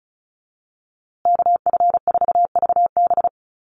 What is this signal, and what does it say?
Morse code, heard as a keyed tone, about 34 wpm, 700 Hz. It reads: KF4VB
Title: USA Call Signs Repeated 2x 35wpm